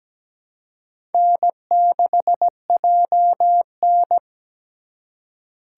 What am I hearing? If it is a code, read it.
Morse code, heard as a keyed tone, about 20 wpm, 700 Hz. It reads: N6JN
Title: Speed Racing - USA Call Signs 17wpm